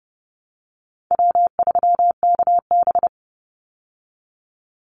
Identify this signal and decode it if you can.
Morse code, heard as a keyed tone, about 30 wpm, 700 Hz. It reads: W3KB
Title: Easy Call Signs Repeated 2x - 30wpm